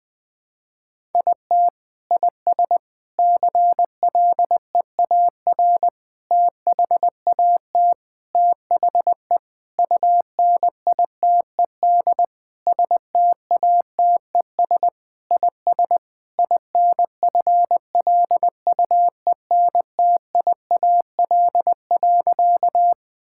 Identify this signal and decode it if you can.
Morse code, heard as a keyed tone, about 20 wpm, 700 Hz. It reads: IT IS CLEAR THAT THE UNITED STATES IS INFLUENTIAL.
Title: Sentences with 4-Word Phrases - 20wpm